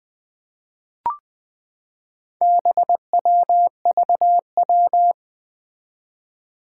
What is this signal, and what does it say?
Morse code, heard as a keyed tone, about 20 wpm, 700 Hz. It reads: BWVW
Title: Four Challenging Letter-Number Combinations 20wpm